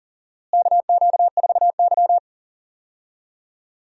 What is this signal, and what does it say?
Morse code, heard as a keyed tone, about 40 wpm, 700 Hz. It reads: KQ4Y
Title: USA Call Signs - 4 Characters - Repeated 2x - 40wpm